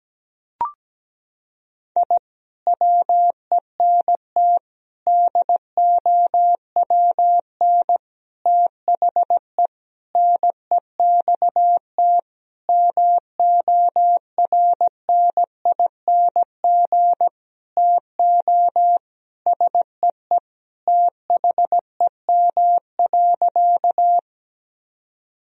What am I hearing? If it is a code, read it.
Morse code, heard as a keyed tone, about 17 wpm, 700 Hz. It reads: I WENT DOWN THE NEXT MORNING TO SEE THEM.
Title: Sentences from Top 1000 Words - Review Part 1 - 17wpm